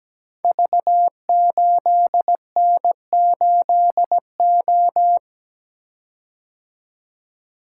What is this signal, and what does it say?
Morse code, heard as a keyed tone, about 17 wpm, 700 Hz. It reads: V8N8O
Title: Code Groups - 5 Letter:Numbers - 17wpm